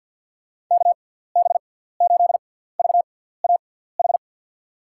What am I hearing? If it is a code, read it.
Morse code, heard as a keyed tone, about 52 wpm, spaced as for 17 wpm, 700 Hz. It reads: KB8VAH
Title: USA Call Signs - 6 Characters - 6x Character Spacing - 50wpm